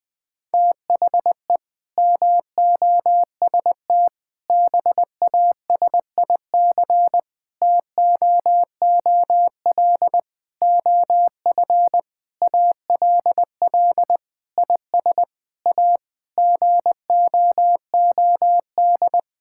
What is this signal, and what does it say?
Morse code, heard as a keyed tone, about 20 wpm, 700 Hz. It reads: THE MOST BASIC TOOL OF ALL IS A GOOD
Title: Sentences from Top 1000 Words - Review Part 6 - 20wpm